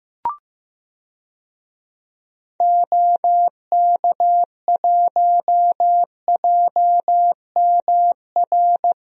Morse code, heard as a keyed tone, about 15 wpm, 700 Hz. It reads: OK1JMR
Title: DXCC-100 Call Signs - 6 Characters - 15wpm